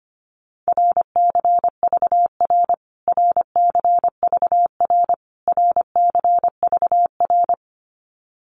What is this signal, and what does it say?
Morse code, heard as a keyed tone, about 25 wpm, 700 Hz. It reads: RCVR RCVR RCVR
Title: Abbreviations Repeated 3x - 25wpm